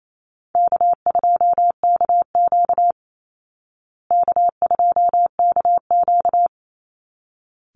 Morse code, heard as a keyed tone, about 28 wpm, 700 Hz. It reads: K2KQ K2KQ
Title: Easy Call Signs Repeated 2x - 28wpm